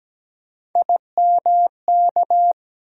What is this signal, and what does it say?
Morse code, heard as a keyed tone, about 17 wpm, 700 Hz. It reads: IMK
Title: USA Call Sign Postfixes - 3 Letters 17wpm